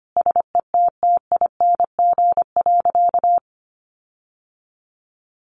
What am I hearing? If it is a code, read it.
Morse code, heard as a keyed tone, about 25 wpm, 700 Hz. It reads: SETTING.
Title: Sentences from Top 1300 Words 25wpm